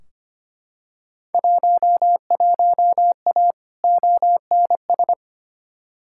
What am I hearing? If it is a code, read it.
Morse code, heard as a keyed tone, about 25 wpm, 700 Hz. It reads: 11A ONS
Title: Field Day Exchanges Encore - 25wpm